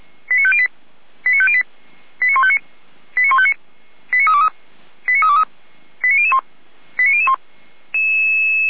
Five-tone selective call (ZVEI), heard as quick runs of five tones, tones of about 70 ms. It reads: 89508 89508 89159 89159 89323 89323 89001 89001